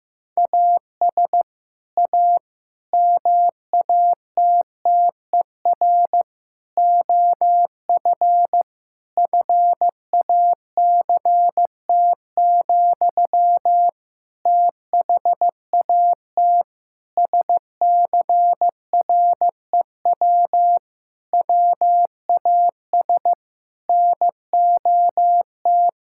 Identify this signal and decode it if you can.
Morse code, heard as a keyed tone, about 15 wpm, 700 Hz. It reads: AS A MATTER OF FACT, THAT SCREW WAS NOT
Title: Sentences from Top 1600 Words 15wpm